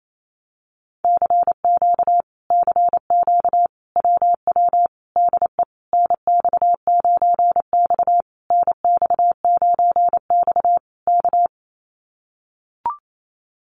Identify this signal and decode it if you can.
Morse code, heard as a keyed tone, about 28 wpm, 700 Hz. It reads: CQ CQ WW DE NX9X NX9X K
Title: Common QSO Phrases Encore 28wpm